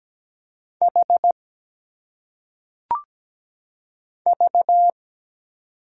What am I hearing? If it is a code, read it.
Morse code, heard as a keyed tone, about 17 wpm, 700 Hz. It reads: H V
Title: Single Letters - Phonetic Pronunciation 17wpm